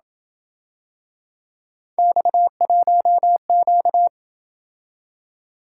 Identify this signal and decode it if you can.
Morse code, heard as a keyed tone, about 27 wpm, 700 Hz. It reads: X1Q